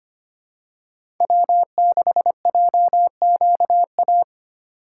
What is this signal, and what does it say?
Morse code, heard as a keyed tone, about 25 wpm, 700 Hz. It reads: W6JQA
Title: USA Call Signs - 5 Characters - 6x Character Spacing - 25wpm